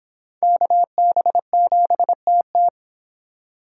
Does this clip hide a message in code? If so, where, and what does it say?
Yes: Morse code from 0.4 to 2.7 s, heard as a keyed tone, about 26 wpm, 700 Hz. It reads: KB7TT